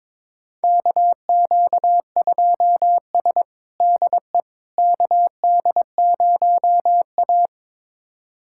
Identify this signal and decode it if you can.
Morse code, heard as a keyed tone, about 22 wpm, 700 Hz. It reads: KQ2S DE KD0A
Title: USA Callsign DE Callsign - 4 Characters - 22wpm